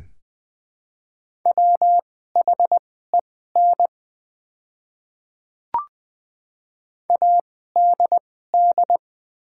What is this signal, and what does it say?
Morse code, heard as a keyed tone, about 20 wpm, spaced as for 15 wpm, 700 Hz. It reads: WHEN ADD
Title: 2x Character Spacing - Top 200 Words - 20wpm